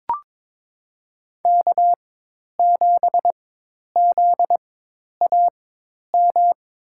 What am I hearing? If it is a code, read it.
Morse code, heard as a keyed tone, about 22 wpm, spaced as for 10 wpm, 700 Hz. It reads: K7ZAM
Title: USA Call Signs - 5 Characters - 4x Character Spacing - 22wpm